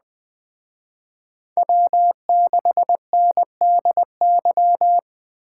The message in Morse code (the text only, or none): W6NDY